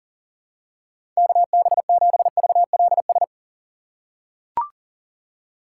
Morse code, heard as a keyed tone, about 40 wpm, 700 Hz. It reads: KB7VLS
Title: USA Call Signs - 6 Characters - 2x Character Spacing - 40wpm